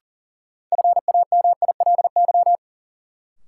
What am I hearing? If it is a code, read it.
Morse code, heard as a keyed tone, about 40 wpm, 700 Hz. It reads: FAMILY
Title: Sentences from Top 400 Words - New Word Drill - 40wpm